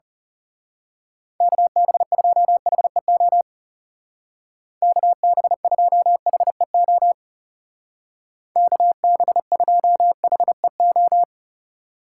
Morse code, heard as a keyed tone, about 34 wpm, 700 Hz. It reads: KB2HEO KB2HEO KB2HEO
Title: Speed Racing - USA 2x3 Call Signs 30wpm